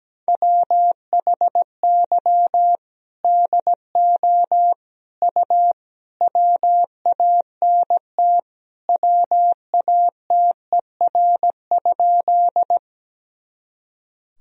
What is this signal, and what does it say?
Morse code, heard as a keyed tone, about 17 wpm, 700 Hz. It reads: WHY DO U WANT WATER?
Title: Sentences from Top 200 Words and Abbreviations 17wpm